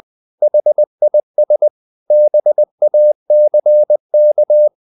Morse code, heard as a keyed tone, about 20 wpm, 600 Hz. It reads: HIS BACK
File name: the old man and the sea part 2 20wpm